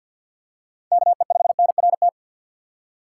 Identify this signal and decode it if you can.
Morse code, heard as a keyed tone, about 53 wpm, 700 Hz. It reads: K E 5 N R T
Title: USA Call Signs - 6 Characters - 8x Character Spacing - 50wpm